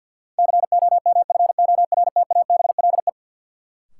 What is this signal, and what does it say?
Morse code, heard as a keyed tone, about 53 wpm, 700 Hz. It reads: COMFORTABLE
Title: Sentences from Top 1500 Words - New Word Drill Alt - 50wpm